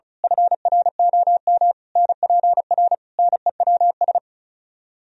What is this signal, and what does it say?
Morse code, heard as a keyed tone, about 33 wpm, 700 Hz. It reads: FROM NPR NEWS
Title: Speed-Racing - Sets of 5 Words 35wpm